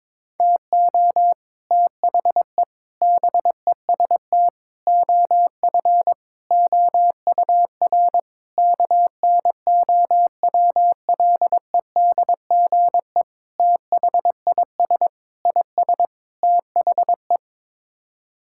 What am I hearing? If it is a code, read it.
Morse code, heard as a keyed tone, about 21 wpm, 700 Hz. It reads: TO THE BEST OF OUR KNOWLEDGE THIS IS THE
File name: Sets of 9 Words - 22wpm